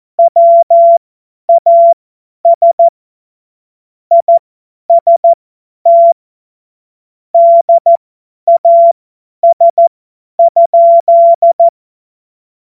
Morse code, heard as a keyed tone, about 14 wpm, spaced as for 10 wpm, 680 Hz. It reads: WAS IST DAS?
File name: Morsekurs / Witze / Witz 07.November 2023 50 F70